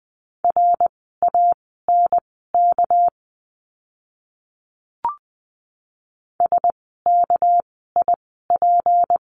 Morse code, heard as a keyed tone, about 20 wpm, spaced as for 15 wpm, 700 Hz. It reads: RANK SKIP
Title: 199 Lesson 32f - Letter k - Words 20wpm